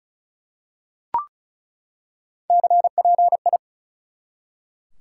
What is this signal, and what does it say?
Morse code, heard as a keyed tone, about 36 wpm, 700 Hz. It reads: CPI
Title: Top 180 QSO Elements 35wpm